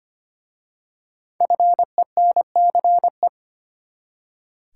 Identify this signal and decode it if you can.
Morse code, heard as a keyed tone, about 25 wpm, 700 Hz. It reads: FENCE
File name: practice / 25wpm / Easy Words with Hints - 25wpm